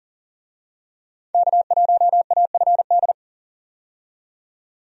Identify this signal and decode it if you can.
Morse code, heard as a keyed tone, about 40 wpm, 700 Hz. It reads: K1AFD
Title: USA Call Signs - 5 Characters - 40wpm